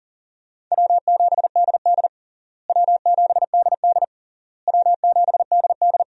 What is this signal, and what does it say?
Morse code, heard as a keyed tone, about 40 wpm, 700 Hz. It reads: W7DD W7DD W7DD
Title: USA Call Signs - 4 Characters - Repeated 3x - 40wpm